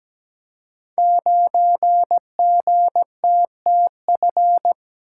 Morse code, heard as a keyed tone, about 17 wpm, 690 Hz. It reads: 9GTTF
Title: Code Groups - 5 Letter:Numbers - 17wpm